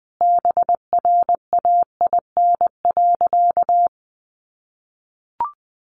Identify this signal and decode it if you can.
Morse code, heard as a keyed tone, about 20 wpm, 700 Hz. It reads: BRAIN.
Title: Sentences from Top 1100 Words 20wpm